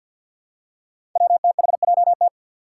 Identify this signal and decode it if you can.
Morse code, heard as a keyed tone, about 47 wpm, 700 Hz. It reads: WTHJT
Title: Five Challenging Letter-Number Combinations 50wpm